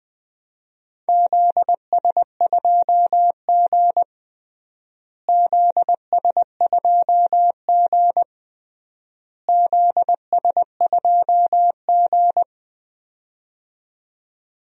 Morse code, heard as a keyed tone, about 21 wpm, 700 Hz. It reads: ZS2G ZS2G ZS2G
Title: DXCC-100 Call Signs - 4 Characters - Repeated 3x - 20wpm